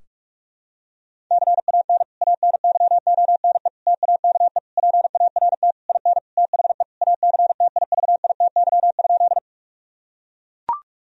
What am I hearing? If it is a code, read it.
Morse code, heard as a keyed tone, about 42 wpm, 700 Hz. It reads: CAN ANYONE TAKE PART IN THE ACTIVITY?
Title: Sentences from Top 1000 Words - Review Part 4 - 45wpm